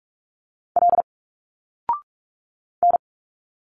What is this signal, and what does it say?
Morse code, heard as a keyed tone, about 46 wpm, 700 Hz. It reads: L N
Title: Single Letters 45wpm